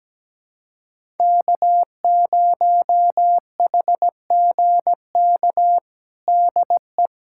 Morse code, heard as a keyed tone, about 17 wpm, 700 Hz. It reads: K0HGK DE
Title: USA Callsign DE Callsign - 5 Characters - 17wpm